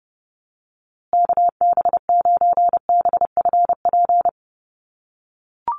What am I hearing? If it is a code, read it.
Morse code, heard as a keyed tone, about 30 wpm, 700 Hz. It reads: KB9BFP
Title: USA Call Signs Repeated 3x 30wpm